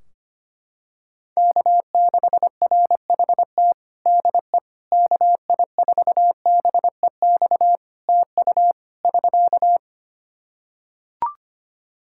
Morse code, heard as a keyed tone, about 25 wpm, 700 Hz. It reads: K6RHT DE KI4BEX TU <SK>
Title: K1USN SST QSO Practice 25wpm